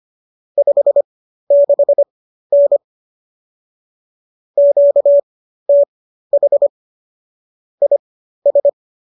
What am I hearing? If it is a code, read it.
Morse code, heard as a keyed tone, about 25 wpm, spaced as for 13 wpm, 570 Hz. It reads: 56N QTH IS